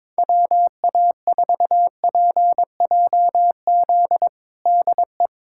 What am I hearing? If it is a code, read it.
Morse code, heard as a keyed tone, about 22 wpm, 700 Hz. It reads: WA4PJZ DE
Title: USA Callsign DE Callsign 22wpm